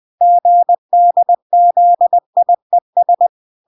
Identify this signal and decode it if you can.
Morse code, heard as a keyed tone, about 20 wpm, 700 Hz. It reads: GDZIES